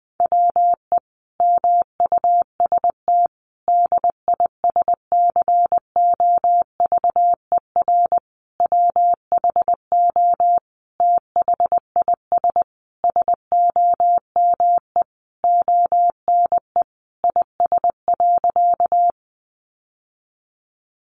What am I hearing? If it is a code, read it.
Morse code, heard as a keyed tone, about 20 wpm, 700 Hz. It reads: WE MUST DISCOVER WHO THIS SOME ONE IS.
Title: Sentences from Top 1100 Words 20wpm